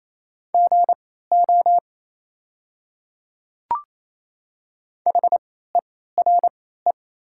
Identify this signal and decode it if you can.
Morse code, heard as a keyed tone, about 28 wpm, spaced as for 16 wpm, 700 Hz. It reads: GO HERE
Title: 3x Character Spacing - Top 500 Words - 28wpm